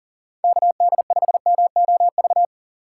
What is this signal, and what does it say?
Morse code, heard as a keyed tone, about 40 wpm, 700 Hz. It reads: KD5MOV
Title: USA Call Signs - 6 Characters - 2x Character Spacing - 40wpm